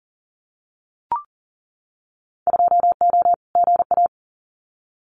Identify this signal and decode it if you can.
Morse code, heard as a keyed tone, about 40 wpm, 700 Hz. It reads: 2O GA
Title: Winter Field Day Exchanges Encore 40wpm